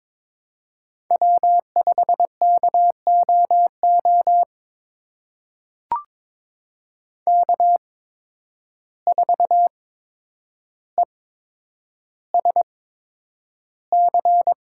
Morse code, heard as a keyed tone, about 22 wpm, spaced as for 9 wpm, 700 Hz. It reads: W5KOO K4ESC
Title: USA Call Signs - 5 Characters - 8x Character Spacing - 22wpm